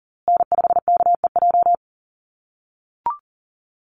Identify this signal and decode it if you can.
Morse code, heard as a keyed tone, about 40 wpm, 700 Hz. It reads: N5KEJ
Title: USA Call Signs Repeated 2x 40wpm